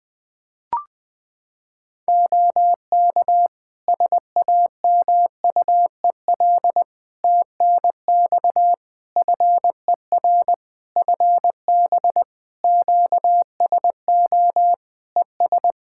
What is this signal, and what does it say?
Morse code, heard as a keyed tone, about 20 wpm, 700 Hz. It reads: OK SAMUEL TNX FER FB QSO ES
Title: K1USN SST QSO Practice 20wpm